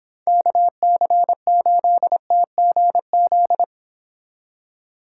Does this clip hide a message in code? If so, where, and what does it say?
Yes: Morse code from 0.3 to 3.6 s, heard as a keyed tone, about 25 wpm, 700 Hz. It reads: KC8TGZ